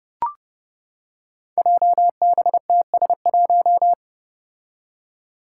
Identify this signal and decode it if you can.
Morse code, heard as a keyed tone, about 30 wpm, 700 Hz. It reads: JBTS1
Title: Five Challenging Letter-Number Combinations 30wpm